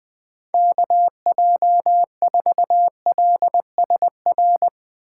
Morse code, heard as a keyed tone, about 21 wpm, 700 Hz. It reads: KJ4LSR
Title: Speed Racing - USA 2x3 Call Signs 20wpm